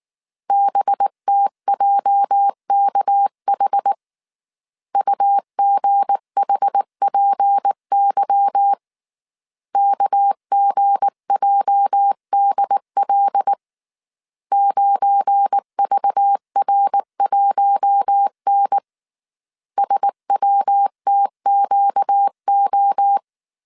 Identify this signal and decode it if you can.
Morse code, heard as a keyed tone, about 19 wpm, 800 Hz. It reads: BTJKH UGHPY KGJDL 9VR1N SWTQO
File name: CW / DET / 1 9 1500C S B 18wpm 2551